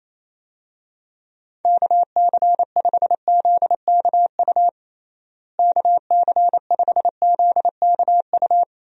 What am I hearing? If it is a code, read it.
Morse code, heard as a keyed tone, about 27 wpm, 700 Hz. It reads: KC5ZKU KC5ZKU
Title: USA Call Signs - 6 Characters - Repeated 3x - 28wpm